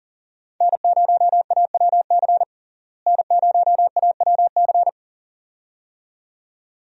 Morse code, heard as a keyed tone, about 37 wpm, 700 Hz. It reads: N0AWC N0AWC
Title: USA Call Signs - 5 Characters - Repeated 2x - 40wpm